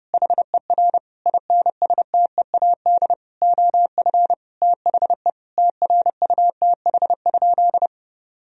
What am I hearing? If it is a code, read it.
Morse code, heard as a keyed tone, about 32 wpm, 700 Hz. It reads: HER INSTEAD OF THE TRUTH?
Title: Sentences from Top 1000 Words - Review Part 4 - 30wpm